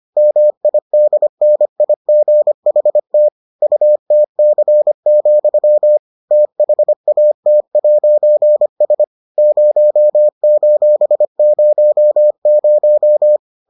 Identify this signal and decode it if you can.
Morse code, heard as a keyed tone, about 25 wpm, 600 Hz. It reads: MIDNIGHT UTC, THAT'S 0:00